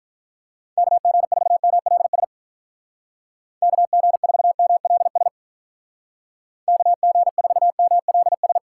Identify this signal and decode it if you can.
Morse code, heard as a keyed tone, about 46 wpm, 700 Hz. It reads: KG4MLS KG4MLS KG4MLS